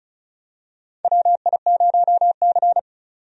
Morse code, heard as a keyed tone, about 36 wpm, 700 Hz. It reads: WI0C
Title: USA Call Signs - 4 Characters - 35wpm